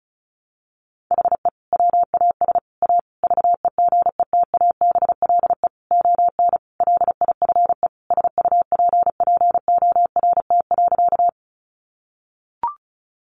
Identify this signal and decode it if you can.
Morse code, heard as a keyed tone, about 34 wpm, 700 Hz. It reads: HE WAS A VEGETABLE ON LIFE SUPPORT.
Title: Sentences from Top 1300 Words 35wpm